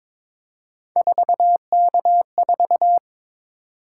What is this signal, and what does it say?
Morse code, heard as a keyed tone, about 22 wpm, 700 Hz. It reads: 4K4
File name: practice / 22wpm / Three Letter-Number Combinations - 22wpm